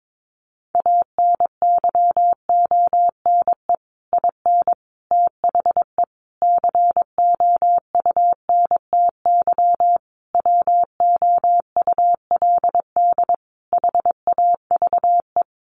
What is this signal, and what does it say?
Morse code, heard as a keyed tone, about 22 wpm, 700 Hz. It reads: ANYONE IN THE COUNTY WOULD HAVE
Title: Sentences from Top 1200 Words 22wpm